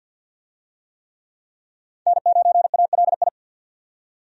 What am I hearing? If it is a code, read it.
Morse code, heard as a keyed tone, about 47 wpm, 700 Hz. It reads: N9ALI
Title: USA Call Signs - 5 Characters - 50wpm